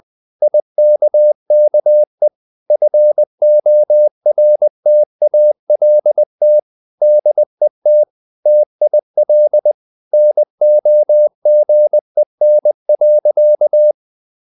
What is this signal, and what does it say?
Morse code, heard as a keyed tone, about 20 wpm, 600 Hz. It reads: IKKE FORTALT DET TIL NOGEN.
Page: Ferie 20 wpm